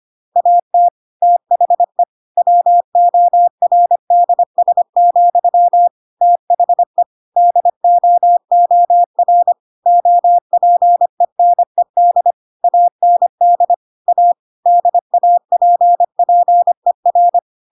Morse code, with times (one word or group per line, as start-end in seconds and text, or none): 0.4-0.9 AT
1.2-2.0 THE
2.4-5.9 WORDS,
6.2-7.0 THE
7.4-9.5 DOOR
9.9-12.3 OPENED
12.6-13.7 AND
14.1-14.3 A
14.7-17.4 DAPPER